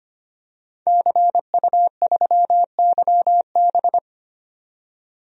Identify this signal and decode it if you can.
Morse code, heard as a keyed tone, about 25 wpm, 700 Hz. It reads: CU3YB
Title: DXCC-100 Call Signs - 5 Characters - 25wpm